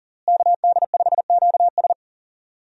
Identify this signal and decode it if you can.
Morse code, heard as a keyed tone, about 40 wpm, 700 Hz. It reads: KD5QS
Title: USA Call Signs - 5 Characters - Repeated 2x - 40wpm